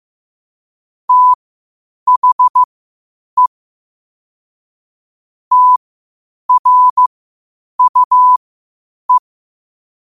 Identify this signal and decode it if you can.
Morse code, heard as a keyed tone, about 15 wpm, spaced as for 8 wpm, 1000 Hz. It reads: THE TRUE
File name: qotdfiles / 8wpm